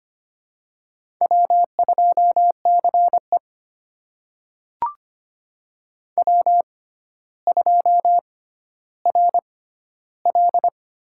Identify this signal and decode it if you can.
Morse code, heard as a keyed tone, about 25 wpm, spaced as for 13 wpm, 700 Hz. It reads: W2CE W2RL
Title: USA Call Signs - 4 Characters - 6x Character Spacing - 25wpm